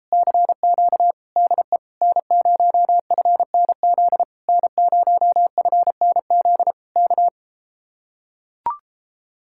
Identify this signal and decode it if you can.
Morse code, heard as a keyed tone, about 33 wpm, 700 Hz. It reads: CQ DE N0FNZ N0FNZ K